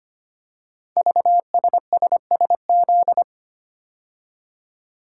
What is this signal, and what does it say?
Morse code, heard as a keyed tone, about 25 wpm, 700 Hz. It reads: VSSSZ
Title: Code Groups - 5 Letters - 25wpm